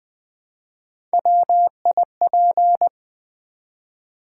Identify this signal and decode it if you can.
Morse code, heard as a keyed tone, about 20 wpm, 700 Hz. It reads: WIP
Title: Build A Word - 1 Letter At A Time - 4-Letter Words - 20wpm